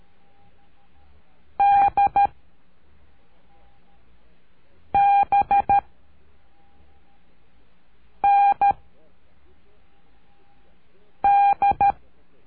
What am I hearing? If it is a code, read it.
Morse code, heard as a keyed tone, about 13 wpm, 800 Hz. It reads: D B N D
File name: cursos / CURSO DE TELEGRAFIA / Telegrafia - Trilha 03